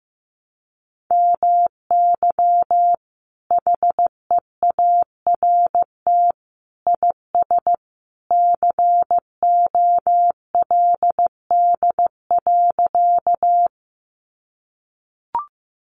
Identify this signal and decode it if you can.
Morse code, heard as a keyed tone, about 15 wpm, 700 Hz. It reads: MY HEART IS COLD.
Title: Sentences from Top 1200 Words 15wpm